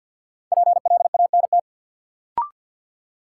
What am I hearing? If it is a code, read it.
Morse code, heard as a keyed tone, about 47 wpm, 700 Hz. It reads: PLANT
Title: Sentences from Top 1000 Words - Review Part 1 - New Word Drill - 50wpm